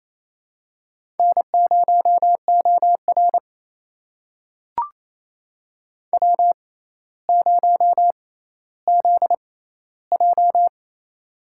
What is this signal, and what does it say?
Morse code, heard as a keyed tone, about 28 wpm, spaced as for 14 wpm, 700 Hz. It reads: N0OR W0ZJ
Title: USA Call Signs - 4 Characters - 6x Character Spacing - 28wpm